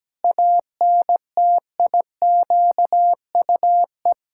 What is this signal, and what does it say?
Morse code, heard as a keyed tone, about 17 wpm, 700 Hz. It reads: ANTIQUE
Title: Pangram Challenge v2 - 17wpm